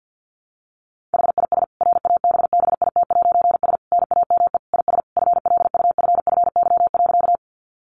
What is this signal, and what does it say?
Morse code, heard as a keyed tone, about 50 wpm, 700 Hz. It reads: HIS RABBIT'S NAME IS FLUFFY.
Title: Easy Phrases 50wpm